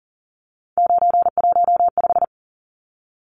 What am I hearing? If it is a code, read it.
Morse code, heard as a keyed tone, about 40 wpm, 700 Hz. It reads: 915
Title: Numbers - 3-Digit 40wpm